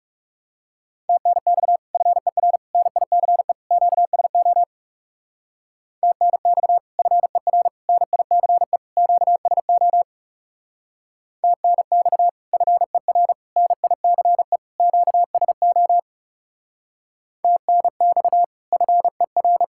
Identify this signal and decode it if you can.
Morse code, heard as a keyed tone, about 35 wpm, 700 Hz. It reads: TNX FER NICE QSO TNX FER NICE QSO TNX FER NICE QSO TNX FER
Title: Speed-Racing Common QSO Phrases - 30wpm